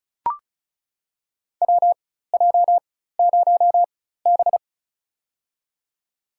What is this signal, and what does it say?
Morse code, heard as a keyed tone, about 36 wpm, spaced as for 16 wpm, 700 Hz. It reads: WJ0B
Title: USA Call Signs - 4 Characters - 4x Character Spacing - 35wpm